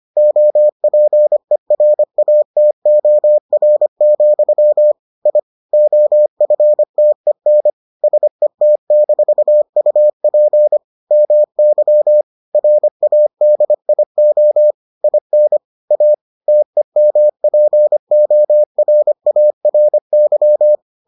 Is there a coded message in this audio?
Yes: Morse code from 0.2 to 20.8 s, heard as a keyed tone, about 25 wpm, 600 Hz. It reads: OPERATOR, I OFTEN SET-UP MY RADIO IN A TEMPORARY